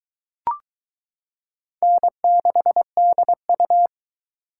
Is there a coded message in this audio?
Yes: Morse code from 1.8 to 3.9 s, heard as a keyed tone, about 22 wpm, 700 Hz. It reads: N6DU